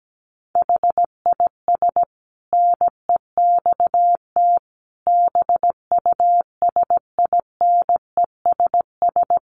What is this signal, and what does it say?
Morse code, heard as a keyed tone, about 17 wpm, 700 Hz. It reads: HIS NEXT BUSINESS
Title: Sentences from Top 1300 Words 17wpm